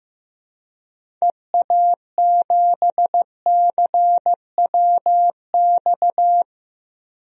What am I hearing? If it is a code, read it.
Morse code, heard as a keyed tone, about 15 wpm, 700 Hz. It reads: EA7CWX
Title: DXCC-100 Call Signs - 6 Characters - Repeated 2x - 15wpm